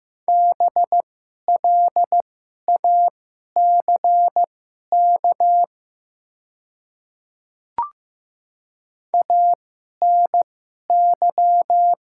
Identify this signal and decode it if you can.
Morse code, heard as a keyed tone, about 15 wpm, spaced as for 11 wpm, 700 Hz. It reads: BLACK ANY
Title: 2x Character Spacing - Top 500 Words - 15wpm